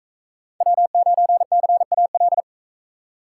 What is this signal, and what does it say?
Morse code, heard as a keyed tone, about 40 wpm, 700 Hz. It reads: W9CAL